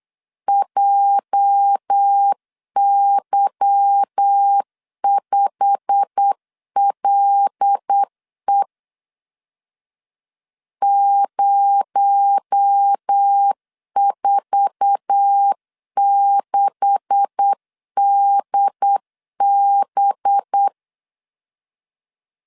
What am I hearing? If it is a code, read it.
Morse code, heard as a keyed tone, about 8 wpm, 800 Hz. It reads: JY5LE 046DB
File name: LEZIONI / 9 286 8wpm 3425